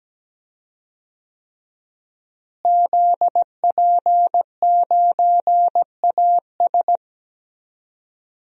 Morse code, heard as a keyed tone, about 17 wpm, 700 Hz. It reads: ZP9AS
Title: DXCC-100 Call Signs - 5 Characters - Repeated 3x - 17wpm